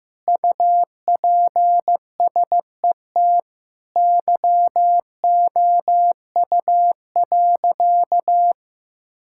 Morse code, heard as a keyed tone, about 15 wpm, 700 Hz. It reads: UPSET YOU.
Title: Sentences from Top 1000 Words - Review Part 5 - 15wpm